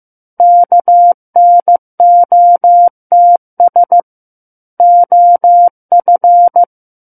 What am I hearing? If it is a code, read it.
Morse code, heard as a keyed tone, about 15 wpm, 700 Hz. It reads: KNOTS OF
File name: Book One / 06-Chapter 6 - 15 WPM